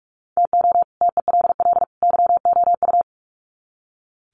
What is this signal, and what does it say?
Morse code, heard as a keyed tone, about 44 wpm, 700 Hz. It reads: TO TELL YOU
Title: Sets of 3 Words 45wpm